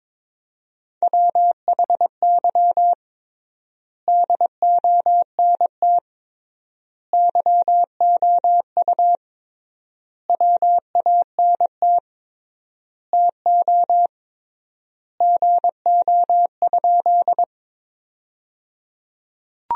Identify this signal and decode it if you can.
Morse code, heard as a keyed tone, about 22 wpm, 700 Hz. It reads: WHY DONT YOU WANT TO GO?
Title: 3x Word-Spacing - Sentences from Top 300 Words - 22wpm